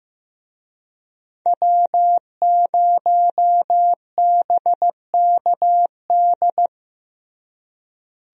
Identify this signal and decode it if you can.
Morse code, heard as a keyed tone, about 15 wpm, 700 Hz. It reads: W0BKD
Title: USA Call Signs - 5 Characters - Repeated 2x - 15wpm